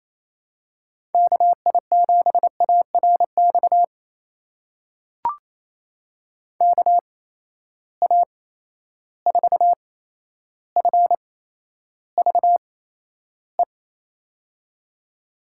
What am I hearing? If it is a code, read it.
Morse code, heard as a keyed tone, about 28 wpm, spaced as for 12 wpm, 700 Hz. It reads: KI7ARX KA4FVE
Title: USA Call Signs - 6 Characters - 8x Character Spacing - 28wpm